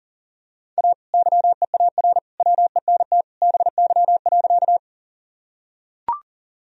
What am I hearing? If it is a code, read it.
Morse code, heard as a keyed tone, about 40 wpm, 700 Hz. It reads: A YEAR WENT BY.
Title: Sentences from Top 1000 Words - Review Part 1 - 40wpm